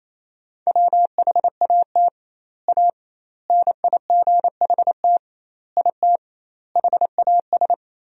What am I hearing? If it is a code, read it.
Morse code, heard as a keyed tone, about 28 wpm, 700 Hz. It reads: WHAT A NIGHT IT HAS
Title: 2x Word-Spacing - Sentences from Top 300 Words - 28wpm